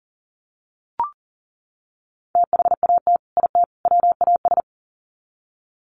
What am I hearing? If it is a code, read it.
Morse code, heard as a keyed tone, about 40 wpm, 700 Hz. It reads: THAT IT WAS
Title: Sets of 3 Words 40wpm